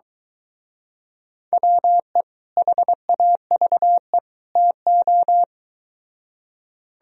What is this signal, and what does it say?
Morse code, heard as a keyed tone, about 23 wpm, 700 Hz. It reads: WE HAVE TO